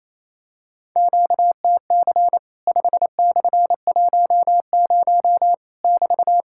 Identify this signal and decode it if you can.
Morse code, heard as a keyed tone, about 27 wpm, 700 Hz. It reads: QTC 5/10 <BT>
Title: Worked All Europe DX Contest - QTC Exchange - 28wpm